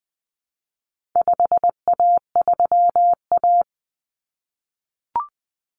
Morse code, heard as a keyed tone, about 20 wpm, 700 Hz. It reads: 5A3A
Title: Common Cut Numbers - 4-Digit 20wpm